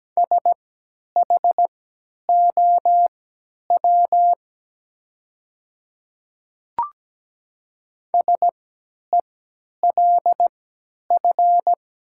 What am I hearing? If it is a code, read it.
Morse code, heard as a keyed tone, about 17 wpm, spaced as for 10 wpm, 700 Hz. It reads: SHOW SELF
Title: 3x Character Spacing - Top 500 Words - 17wpm